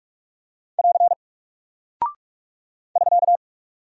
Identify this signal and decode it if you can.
Morse code, heard as a keyed tone, about 44 wpm, 700 Hz. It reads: <AR> <SK>
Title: Single Character (Letter:Number:Punctuation:Prosign) Alt 45wpm